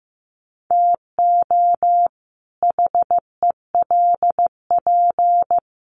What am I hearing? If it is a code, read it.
Morse code, heard as a keyed tone, about 15 wpm, 700 Hz. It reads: TO HELP
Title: Sets of 2 Words 15wpm